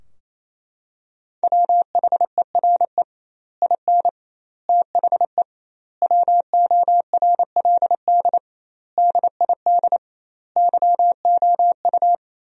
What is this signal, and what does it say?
Morse code, heard as a keyed tone, about 28 wpm, 700 Hz. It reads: WHERE IN THE WORLD DID YOU
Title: 2x Word-Spacing - Sentences from Top 200 Words - 28wpm